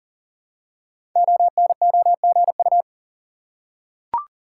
Morse code, heard as a keyed tone, about 40 wpm, 690 Hz. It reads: ONOGU
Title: Code Groups - 5 Letters - 40wpm